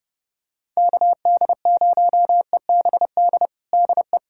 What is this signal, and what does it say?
Morse code, heard as a keyed tone, about 32 wpm, 700 Hz. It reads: KD0EBD DE
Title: USA Callsign DE Callsign - 6 Characters - 30wpm